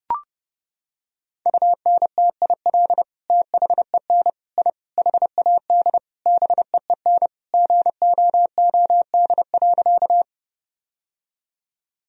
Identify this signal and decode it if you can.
Morse code, heard as a keyed tone, about 30 wpm, 700 Hz. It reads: UNTIL THEN I HAD BEEN GOOD.
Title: Sentences from Top 1000 Words - Review Part 1 - 30wpm